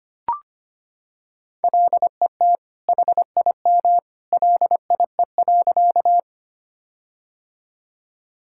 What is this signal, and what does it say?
Morse code, heard as a keyed tone, about 25 wpm, 700 Hz. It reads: LET HIM LIE.
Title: Sentences from Top 1600 Words 25wpm